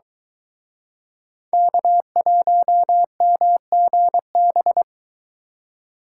Morse code, heard as a keyed tone, about 22 wpm, 700 Hz. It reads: K1MGB